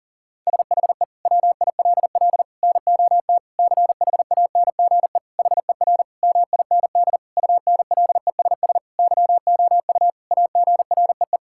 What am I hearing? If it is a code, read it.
Morse code, heard as a keyed tone, about 40 wpm, 700 Hz. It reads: SHE WILL NOT CHANGE HER MIND UNLESS YOU AGREE
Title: Sentences from Top 1000 Words - Review Part 6 - 40wpm